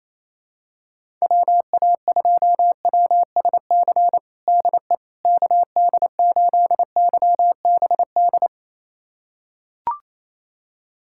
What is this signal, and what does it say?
Morse code, heard as a keyed tone, about 27 wpm, 700 Hz. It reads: WA2WSC DE KD8YBD
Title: USA Callsign DE Callsign - 6 Characters - 28wpm